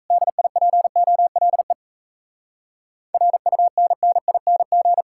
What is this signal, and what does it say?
Morse code, heard as a keyed tone, about 38 wpm, 700 Hz. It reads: DIPOLE RUNNING